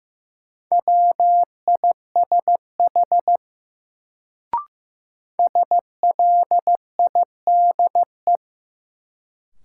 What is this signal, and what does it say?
Morse code, heard as a keyed tone, about 15 wpm, 700 Hz. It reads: WISH SLIDE
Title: Sentences from Top 1600 Words - New Word Drill - 15wpm